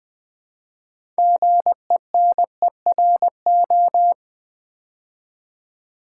Morse code, heard as a keyed tone, about 20 wpm, 700 Hz. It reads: GENERO
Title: Build A Word - 2 Letters At A Time - 8-Letter Words - 20wpm